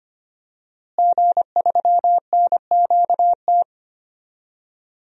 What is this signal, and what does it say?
Morse code, heard as a keyed tone, about 25 wpm, 700 Hz. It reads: G3NQT
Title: DXCC-100 Call Signs - 5 Characters - Repeated 2x - 25wpm